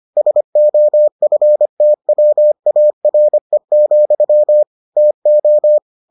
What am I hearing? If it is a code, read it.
Morse code, heard as a keyed tone, about 25 wpm, 600 Hz. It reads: SOFTWARE, TO